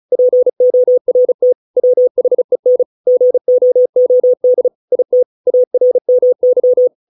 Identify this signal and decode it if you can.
Morse code, heard as a keyed tone, about 35 wpm, 500 Hz. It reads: PORT WHEN GOOD IT ARMY